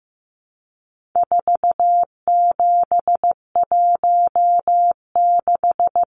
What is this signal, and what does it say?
Morse code, heard as a keyed tone, about 15 wpm, 700 Hz. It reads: 4716
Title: Numbers - 4-Digit 15wpm